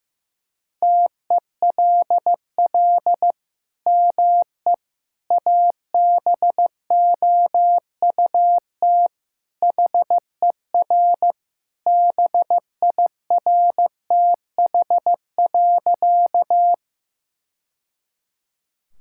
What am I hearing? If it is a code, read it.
Morse code, heard as a keyed tone, about 15 wpm, 700 Hz. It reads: TELL ME ABOUT HER BIRTH.
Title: Sentences from Top 1000 Words - Review Part 5 - 15wpm